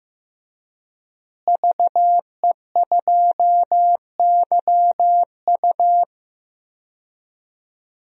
Morse code, heard as a keyed tone, about 15 wpm, 700 Hz. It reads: VE2YU
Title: DXCC-100 Call Signs - 5 Characters - Repeated 2x - 15wpm